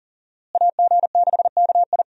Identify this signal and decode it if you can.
Morse code, heard as a keyed tone, about 40 wpm, 700 Hz. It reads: AG6KI
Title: USA Call Signs - 5 Characters - Repeated 3x - 40wpm